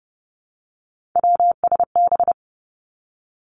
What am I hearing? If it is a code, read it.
Morse code, heard as a keyed tone, about 32 wpm, 700 Hz. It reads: WSB
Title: Three Challenging Letter-Number Combinations 30wpm